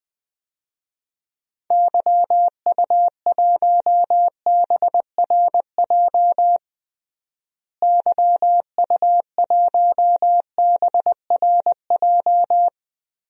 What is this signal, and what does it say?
Morse code, heard as a keyed tone, about 20 wpm, 700 Hz. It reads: YU1BRJ YU1BRJ
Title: DXCC-100 Call Signs - 6 Characters - Repeated 2x - 20wpm